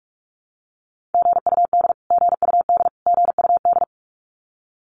Morse code, heard as a keyed tone, about 44 wpm, 700 Hz. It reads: GUD GUD GUD
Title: Abbreviations Repeated 3x - 45wpm